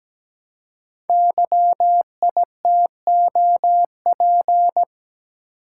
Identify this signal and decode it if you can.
Morse code, heard as a keyed tone, about 17 wpm, 700 Hz. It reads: YITOP
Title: Code Groups - 5 Letters - 17wpm